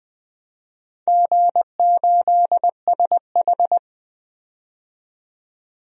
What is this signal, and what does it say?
Morse code, heard as a keyed tone, about 20 wpm, 700 Hz. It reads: G8SH
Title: DXCC-100 Call Signs - 4 Characters - 20wpm